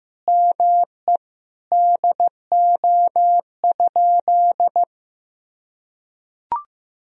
Morse code, heard as a keyed tone, about 15 wpm, 700 Hz. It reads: ME DO?
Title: Sentences from Top 200 Words and Abbreviations 15wpm